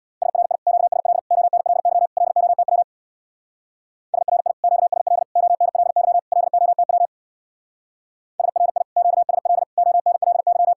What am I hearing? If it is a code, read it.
Morse code, heard as a keyed tone, about 60 wpm, 700 Hz. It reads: SRI QSL ONLY LOTW SRI QSL ONLY LOTW SRI QSL ONLY
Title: Speed-Racing Common QSO Phrases Encore - 50wpm